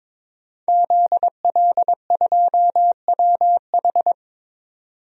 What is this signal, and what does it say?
Morse code, heard as a keyed tone, about 22 wpm, 700 Hz. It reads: ZL2WH
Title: DXCC-100 Call Signs - 5 Characters - 22wpm